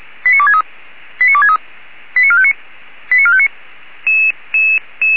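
Five-tone selective call (ZVEI), heard as quick runs of five tones, tones of about 70 ms. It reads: 89263 89263 89459 89459
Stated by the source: ZVEI five-tone selective call 89269, received on SDR.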